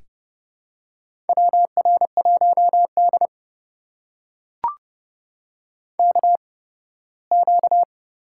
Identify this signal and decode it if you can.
Morse code, heard as a keyed tone, about 30 wpm, spaced as for 18 wpm, 700 Hz. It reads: WR1D KQ
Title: USA Call Signs - 4 Characters - 8x Character Spacing - 30wpm